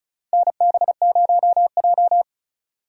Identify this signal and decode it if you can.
Morse code, heard as a keyed tone, about 36 wpm, 700 Hz. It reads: NB0J
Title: USA Call Signs - 4 Characters - 4x Character Spacing - 35wpm